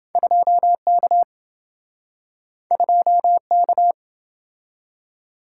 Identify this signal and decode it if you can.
Morse code, heard as a keyed tone, about 27 wpm, 700 Hz. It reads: 2K 2K